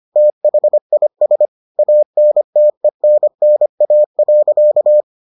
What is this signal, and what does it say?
Morse code, heard as a keyed tone, about 25 wpm, 600 Hz. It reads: THIS ANTENNA.